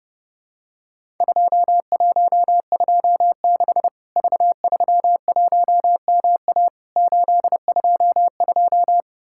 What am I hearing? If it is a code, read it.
Morse code, heard as a keyed tone, about 28 wpm, 700 Hz. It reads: 2126 V31MA 822
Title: Worked All Europe DX Contest - QTC Exchange - 30wpm